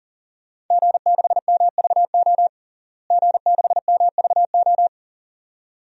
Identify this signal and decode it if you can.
Morse code, heard as a keyed tone, about 40 wpm, 700 Hz. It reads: G6MVO G6MVO
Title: DXCC-100 Call Signs - 5 Characters - Repeated 2x - 40wpm